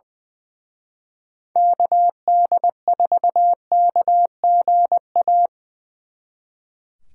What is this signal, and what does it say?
Morse code, heard as a keyed tone, about 25 wpm, 700 Hz. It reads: KD4KGA
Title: Speed Racing - USA 2x3 Call Signs 20wpm